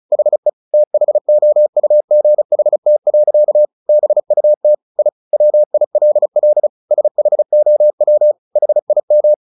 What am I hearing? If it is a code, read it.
Morse code, heard as a keyed tone, about 34 wpm, 600 Hz. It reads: HE THOUGHT. BUT I WILL SHOW HIM
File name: the old man and the sea / the old man and the sea 1 35wpm